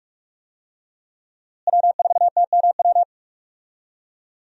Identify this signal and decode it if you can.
Morse code, heard as a keyed tone, about 46 wpm, 700 Hz. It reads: W4TMW
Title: USA Call Signs - 5 Characters - Repeated 2x - 45wpm